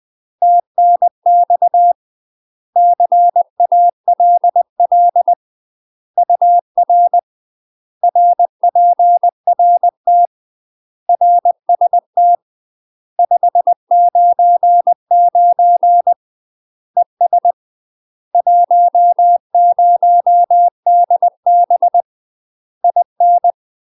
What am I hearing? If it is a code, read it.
Morse code, heard as a keyed tone, about 20 wpm, 700 Hz. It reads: TNX CALL UR RPRT RST 599 ES 10DB IN
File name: UK MASTER QSOs TRAINING / 20WPM / UK MASTER QSOs 0028